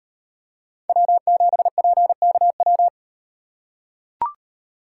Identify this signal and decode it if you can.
Morse code, heard as a keyed tone, about 38 wpm, 700 Hz. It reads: W7PKW